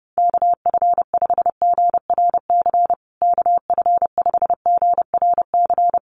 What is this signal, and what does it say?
Morse code, heard as a keyed tone, about 30 wpm, 700 Hz. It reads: KF5GRC KF5GRC
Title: USA Call Signs Repeated 2x 30wpm